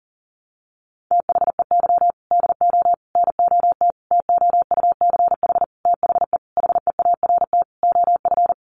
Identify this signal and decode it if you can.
Morse code, heard as a keyed tone, about 40 wpm, 700 Hz. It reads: THEY DO NOT TOUCH THE HEART OF
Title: Sentences from Top 1200 Words 40wpm